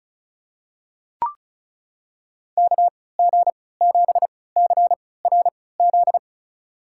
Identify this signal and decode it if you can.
Morse code, heard as a keyed tone, about 36 wpm, spaced as for 20 wpm, 700 Hz. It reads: KG7CRZ
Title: USA Call Signs - 3x Character Spacing - 35wpm